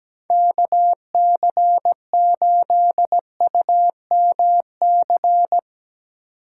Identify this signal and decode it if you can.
Morse code, heard as a keyed tone, about 17 wpm, 700 Hz. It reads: K C 8 U M C
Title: USA Call Signs - 6 Characters - 8x Character Spacing - 17wpm